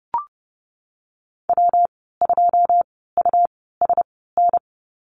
Morse code, heard as a keyed tone, about 31 wpm, spaced as for 17 wpm, 700 Hz. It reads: W2USN
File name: Lesson 19g - Letter w - Call Signs 30wpm